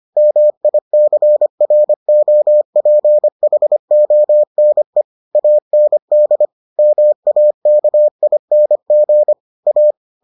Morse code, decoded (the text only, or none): MICROPHONE AND MAKING A